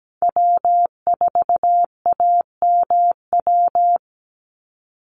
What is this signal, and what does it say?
Morse code, heard as a keyed tone, about 17 wpm, 700 Hz. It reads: W4AMW
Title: NA SOTA Chaser Callsigns 17wpm